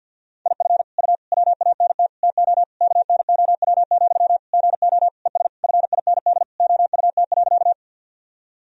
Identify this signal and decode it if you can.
Morse code, heard as a keyed tone, about 46 wpm, 700 Hz. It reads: IF U WANT TO KNOW, GO ES FIND OUT.
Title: Sentences from Top 200 Words and Abbreviations 50wpm